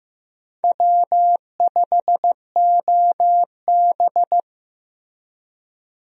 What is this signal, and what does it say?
Morse code, heard as a keyed tone, about 15 wpm, 700 Hz. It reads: W5OB
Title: USA Call Signs - 4 Characters - Repeated 2x - 15wpm